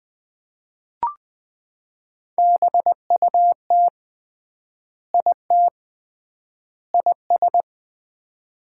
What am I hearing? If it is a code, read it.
Morse code, heard as a keyed tone, about 20 wpm, 700 Hz. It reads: BUT IT IS
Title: 3x Word-Spacing - Sets of 3 Words - 20wpm